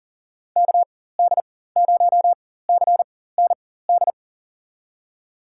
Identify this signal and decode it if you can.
Morse code, heard as a keyed tone, about 38 wpm, spaced as for 18 wpm, 700 Hz. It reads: KD0CND
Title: USA Call Signs - 6 Characters - 4x Character Spacing - 40wpm